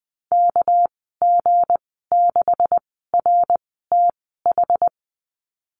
Morse code, heard as a keyed tone, about 20 wpm, spaced as for 14 wpm, 700 Hz. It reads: KG6RTH
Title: 245 Lesson 39g - Letter z - Call Signs 20wpm